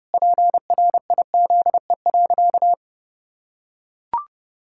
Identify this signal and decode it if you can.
Morse code, heard as a keyed tone, about 30 wpm, 700 Hz. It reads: PRIZE.
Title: Pangram Challenge v2 - 30wpm